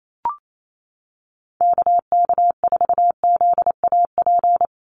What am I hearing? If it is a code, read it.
Morse code, heard as a keyed tone, about 28 wpm, 700 Hz. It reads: KK4ZAP
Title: USA Call Signs Repeated 2x 28wpm